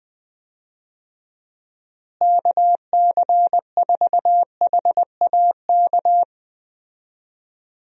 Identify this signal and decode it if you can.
Morse code, heard as a keyed tone, about 20 wpm, 700 Hz. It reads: KC4HAK
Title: USA Call Signs - 6 Characters - Repeated 2x - 20wpm